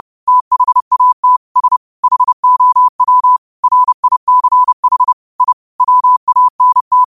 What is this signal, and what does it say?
Morse code, heard as a keyed tone, about 30 wpm, 1000 Hz. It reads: THAT S HOW RICH I WANT